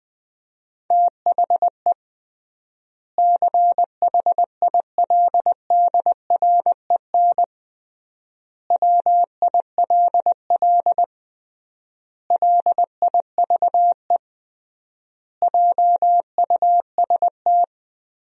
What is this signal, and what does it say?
Morse code, heard as a keyed tone, about 20 wpm, 700 Hz. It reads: THE CHILDREN WILL LIVE JUST
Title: 3x Word-Spacing - Sentences from Top 300 Words - 20wpm